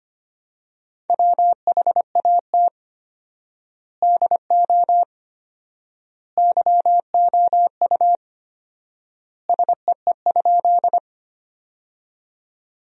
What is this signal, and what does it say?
Morse code, heard as a keyed tone, about 25 wpm, 700 Hz. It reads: WHAT DO YOU SEE?
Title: 4x Word-Spacing - Sentences from Top 100 Words - 25wpm